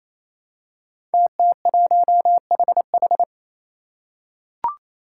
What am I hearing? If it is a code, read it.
Morse code, heard as a keyed tone, about 27 wpm, 700 Hz. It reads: TT1HH
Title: Five Challenging Letter-Number Combinations 28wpm